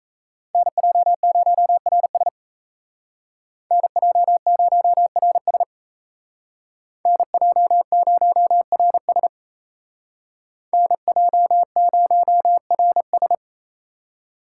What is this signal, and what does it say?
Morse code, heard as a keyed tone, about 35 wpm, 700 Hz. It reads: NJ0RS NJ0RS NJ0RS NJ0RS